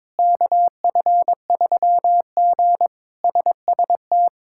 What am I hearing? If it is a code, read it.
Morse code, heard as a keyed tone, about 22 wpm, 700 Hz. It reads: KF3G SST
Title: K1USN SST QSO Practice - 22wpm